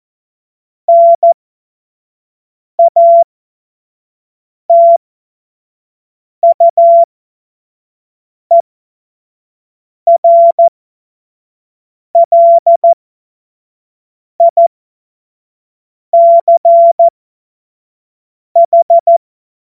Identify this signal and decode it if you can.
Morse code, heard as a keyed tone, about 14 wpm, spaced as for 5 wpm, 680 Hz. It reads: NATUERLICH
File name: Morsekurs / Witze / Witz 21. Dezember 2021, 50 F70